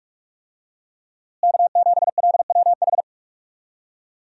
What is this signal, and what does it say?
Morse code, heard as a keyed tone, about 41 wpm, 700 Hz. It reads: K7LWH
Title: USA Call Signs - 5 Characters - Repeated 3x - 45wpm